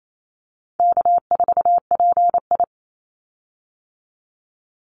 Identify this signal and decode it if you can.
Morse code, heard as a keyed tone, about 28 wpm, 700 Hz. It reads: K4PI
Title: Easy Call Signs Repeated 2x - 28wpm